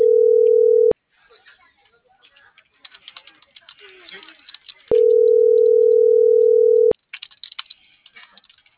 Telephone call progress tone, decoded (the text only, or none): ringback tone